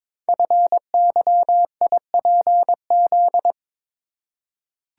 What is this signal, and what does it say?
Morse code, heard as a keyed tone, about 22 wpm, 700 Hz. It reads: FYIPZ
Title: Code Groups - 5 Letter:Numbers - 22wpm